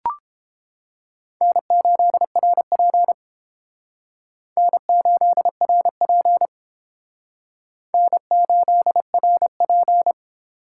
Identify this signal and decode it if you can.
Morse code, heard as a keyed tone, about 30 wpm, 700 Hz. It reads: N8RP N8RP N8RP